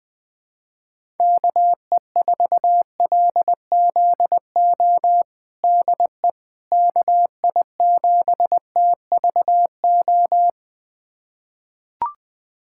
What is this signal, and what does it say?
Morse code, heard as a keyed tone, about 20 wpm, 700 Hz. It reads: KE4LZO DE KI7TVO
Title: USA Callsign DE Callsign - 6 Characters - 20wpm